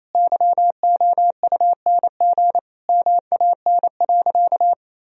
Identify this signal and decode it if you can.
Morse code, heard as a keyed tone, about 28 wpm, 700 Hz. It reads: YOUNG MAN.
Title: Sentences from Top 1000 Words - Review Part 4 - 28wpm